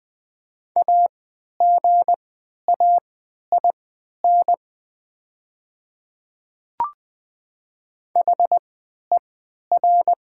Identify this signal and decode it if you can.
Morse code, heard as a keyed tone, about 20 wpm, spaced as for 11 wpm, 700 Hz. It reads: AGAIN HER
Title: 3x Character Spacing - Top 500 Words - 20wpm